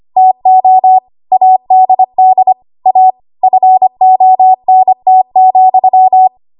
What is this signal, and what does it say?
Morse code, heard as a keyed tone, about 25 wpm, 750 Hz. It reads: TO ADD A FONT,
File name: Archive / 25 WPM / 25WPM